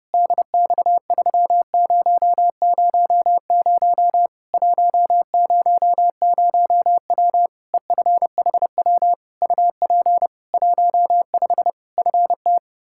Morse code, heard as a keyed tone, about 30 wpm, 700 Hz. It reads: DX3000 100W EFHW UP 15 FT
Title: Practice QSO - Exchange 2 - Info - 30wpm